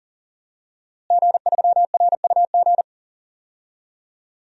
Morse code, heard as a keyed tone, about 40 wpm, 700 Hz. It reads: G3RUG
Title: DXCC-100 Call Signs - 5 Characters - Repeated 2x - 40wpm